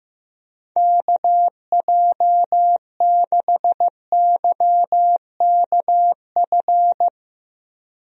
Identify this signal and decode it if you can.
Morse code, heard as a keyed tone, about 15 wpm, 700 Hz. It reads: KJ6YKF
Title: USA Call Signs - 6 Characters - Repeated 2x - 15wpm